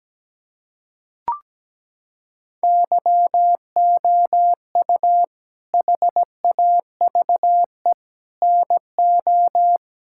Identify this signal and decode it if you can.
Morse code, heard as a keyed tone, about 17 wpm, 700 Hz. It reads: YOU HAVE NO
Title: Sentences from Top 1000 Words - Review Part 5 - 17wpm